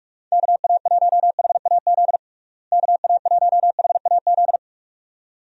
Morse code, heard as a keyed tone, about 46 wpm, 700 Hz. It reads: KA1HAZ KA1HAZ
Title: USA Call Signs - 6 Characters - Repeated 2x - 45wpm